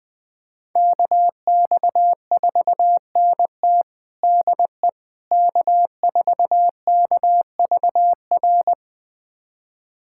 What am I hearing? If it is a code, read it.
Morse code, heard as a keyed tone, about 20 wpm, 700 Hz. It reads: KX4NT DE K4KVR
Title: USA Callsign DE Callsign - 5 Characters - 20wpm